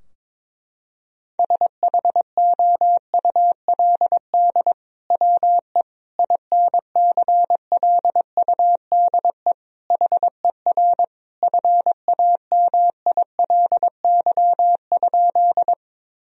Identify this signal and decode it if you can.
Morse code, heard as a keyed tone, about 22 wpm, 700 Hz. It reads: SHOULD WE INCLUDE HER FAMILY?
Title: Sentences from Top 1000 Words - Review Part 3 - 22wpm